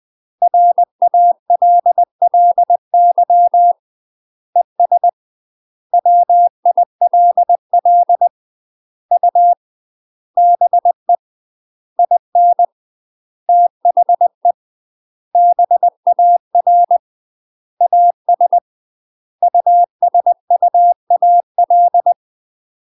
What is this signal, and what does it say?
Morse code, heard as a keyed tone, about 20 wpm, 700 Hz. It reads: RALLY ES WILL U BE IN THE BAR AS USUAL
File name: 20WPM